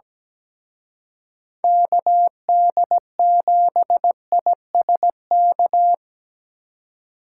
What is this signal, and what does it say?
Morse code, heard as a keyed tone, about 20 wpm, 700 Hz. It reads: KD7ISK
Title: Speed Racing - USA Call Signs 17wpm